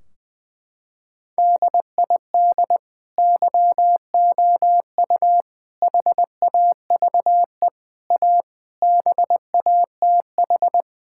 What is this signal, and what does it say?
Morse code, heard as a keyed tone, about 21 wpm, 700 Hz. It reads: DID YOU HAVE A BATH
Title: Sentences from Top 1600 Words 20wpm